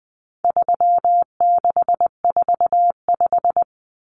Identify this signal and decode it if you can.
Morse code, heard as a keyed tone, about 20 wpm, 700 Hz. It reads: 3645
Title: Common Cut Numbers - 4-Digit 20wpm